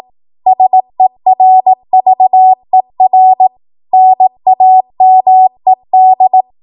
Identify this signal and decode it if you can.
Morse code, heard as a keyed tone, about 18 wpm, 750 Hz. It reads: SERVER NAMED
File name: Archive / 18 WPM / 18WPM